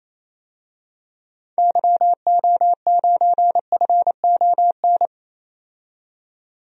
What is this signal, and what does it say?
Morse code, heard as a keyed tone, about 28 wpm, 700 Hz. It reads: YO9FON
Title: DXCC-100 Call Signs - 6 Characters - 28wpm